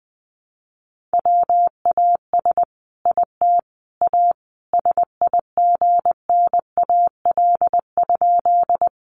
Morse code, heard as a keyed tone, about 20 wpm, 700 Hz. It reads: WAS IT A SIGNAL?
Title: Sentences from Top 1400 Words 20wpm